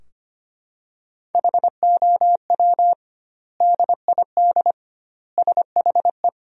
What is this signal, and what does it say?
Morse code, heard as a keyed tone, about 25 wpm, 700 Hz. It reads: HOW DID SHE
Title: 2x Word-Spacing - Sentences from Top 100 Words - 25wpm